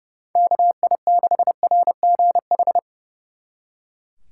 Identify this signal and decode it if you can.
Morse code, heard as a keyed tone, about 36 wpm, 700 Hz. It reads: KI6RGH
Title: Speed Racing - USA 2x3 Call Signs 30wpm